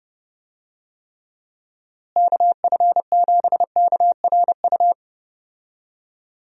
Morse code, heard as a keyed tone, about 30 wpm, 700 Hz. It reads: KF7KRU
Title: USA Call Signs - 6 Characters - Repeated 3x - 30wpm